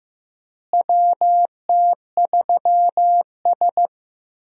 Morse code, heard as a keyed tone, about 15 wpm, 700 Hz. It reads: WT3S
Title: USA Call Signs - 4 Characters - Repeated 3x - 15wpm